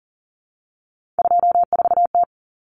Morse code, heard as a keyed tone, about 40 wpm, 700 Hz. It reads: 24T
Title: Common Cut Numbers - 3-Digit 40wpm